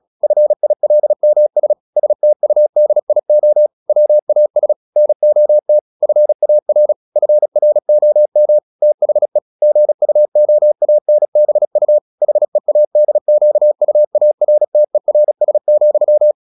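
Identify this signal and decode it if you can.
Morse code, heard as a keyed tone, about 36 wpm, 600 Hz. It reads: FILMS STUDIO WAS NOT FAR FROM THE GUOANBU HEADQUARTERS,
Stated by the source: Morse practice at 35 wpm from CW med Gnister, sending Ken Follett's novel Never in English.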